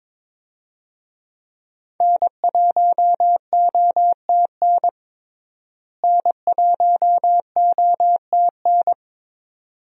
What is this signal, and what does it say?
Morse code, heard as a keyed tone, about 22 wpm, 700 Hz. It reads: N1OTN N1OTN
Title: USA Call Signs - 5 Characters - Repeated 2x - 22wpm